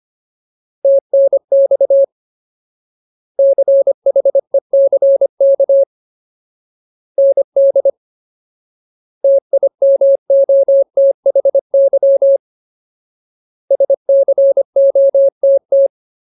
Morse code, heard as a keyed tone, about 25 wpm, 550 Hz. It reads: TNX CHECK ND TIMOTHY SCOTT
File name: Ultimate Word List / 25WPM Ultimate